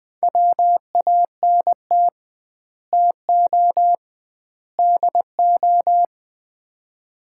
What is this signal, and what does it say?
Morse code, heard as a keyed tone, about 20 wpm, 700 Hz. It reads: WANT TO DO
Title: 2x Word-Spacing - Sets of 3 Words - 20wpm